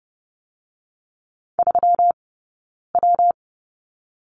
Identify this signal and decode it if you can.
Morse code, heard as a keyed tone, about 30 wpm, spaced as for 9 wpm, 700 Hz. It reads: 3W
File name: Lesson 20d - Number 3 - Two Characters 30wpm